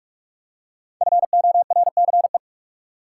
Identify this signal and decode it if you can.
Morse code, heard as a keyed tone, about 45 wpm, 700 Hz. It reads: FORCE
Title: Five Letter Words - 45wpm